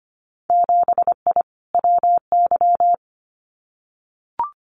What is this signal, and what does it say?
Morse code, heard as a keyed tone, about 25 wpm, 700 Hz. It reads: 7I WY
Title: Winter Field Day Exchanges Encore 25wpm